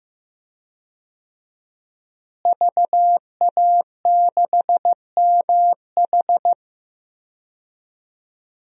Morse code, heard as a keyed tone, about 15 wpm, 700 Hz. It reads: VA6MH
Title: DXCC-100 Call Signs - 5 Characters - Repeated 3x - 15wpm